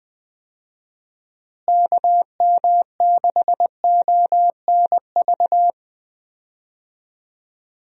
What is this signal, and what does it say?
Morse code, heard as a keyed tone, about 20 wpm, 700 Hz. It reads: KM6ONV
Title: USA Call Signs - 6 Characters - Repeated 3x - 20wpm